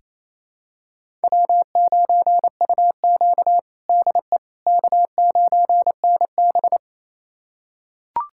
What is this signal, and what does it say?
Morse code, heard as a keyed tone, about 28 wpm, 700 Hz. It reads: W9UQ DE K9NB
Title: USA Callsign DE Callsign - 4 Characters - 28wpm